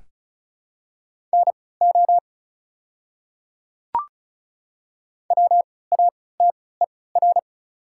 Morse code, heard as a keyed tone, about 35 wpm, spaced as for 20 wpm, 690 Hz. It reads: NO WATER
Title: 3x Character Spacing - Top 100 Words - 35wpm